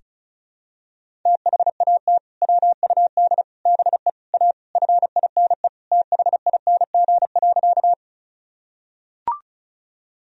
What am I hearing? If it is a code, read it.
Morse code, heard as a keyed tone, about 34 wpm, 700 Hz. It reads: THAT WUD BE A FINE THING.
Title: Sentences from Top 500 Words and Abbreviations 35wpm